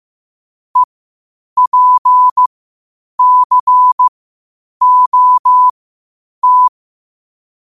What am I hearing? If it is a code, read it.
Morse code, heard as a keyed tone, about 15 wpm, spaced as for 8 wpm, 1000 Hz. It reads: EPCOT
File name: qotdfiles / 8wpm